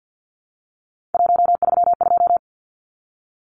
Morse code, heard as a keyed tone, about 50 wpm, 700 Hz. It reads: 132
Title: Numbers - 3-Digit 50wpm